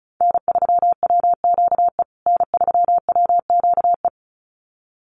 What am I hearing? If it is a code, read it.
Morse code, heard as a keyed tone, about 34 wpm, 700 Hz. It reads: N3WQE N3WQE
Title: USA Call Signs Repeated 2x 35wpm